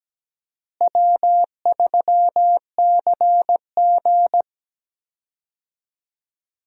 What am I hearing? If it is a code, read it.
Morse code, heard as a keyed tone, about 17 wpm, 700 Hz. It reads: W3CG
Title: USA Call Signs - 4 Characters - Repeated 3x - 17wpm